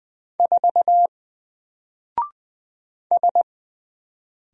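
Morse code, heard as a keyed tone, about 20 wpm, 700 Hz. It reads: 4 S
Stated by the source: Single letter and number Morse drill at 20 wpm, from N4LCD.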